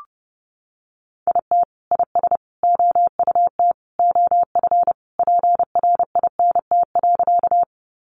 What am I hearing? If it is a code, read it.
Morse code, heard as a keyed tone, about 30 wpm, 700 Hz. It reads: IT IS OUT OF PRINT.
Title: Sentences from Top 1400 Words 30wpm